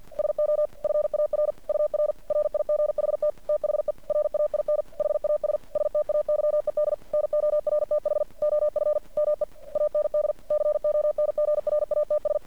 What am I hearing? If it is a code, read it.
Morse code, heard as a keyed tone, about 51 wpm, 600 Hz. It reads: SO FAR UR RIGHT THE RAIN HAS STAYED NORTH OF ME AND CONGRATS
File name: archived / CRSnet